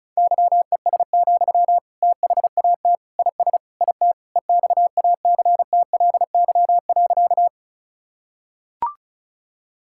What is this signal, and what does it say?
Morse code, heard as a keyed tone, about 34 wpm, 700 Hz. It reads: YES, THAT IS IT EXACTLY.
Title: Sentences from Top 1000 Words - Review Part 6 - 35wpm